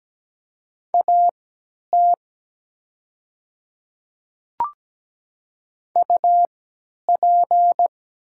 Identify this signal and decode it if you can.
Morse code, heard as a keyed tone, about 17 wpm, spaced as for 10 wpm, 690 Hz. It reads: AT UP
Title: 3x Character Spacing - Top 100 Words - 17wpm